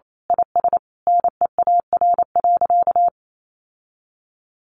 Morse code, heard as a keyed tone, about 28 wpm, 700 Hz. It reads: IS NEAR.
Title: Sentences from Top 700 Words 28wpm